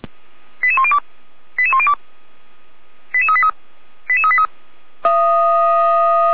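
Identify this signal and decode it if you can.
Five-tone selective call (ZVEI), heard as quick runs of five tones, tones of about 70 ms. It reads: 88102 88102 88383 88383
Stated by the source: ZVEI five-tone selective call 89060 received by SDR.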